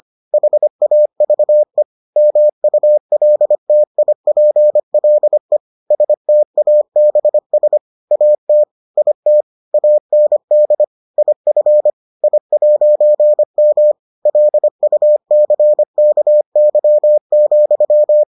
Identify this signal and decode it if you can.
Morse code, heard as a keyed tone, about 25 wpm, 600 Hz. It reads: HAVE MULTIPLE STABS AT IT AND IF I'M LUCKY,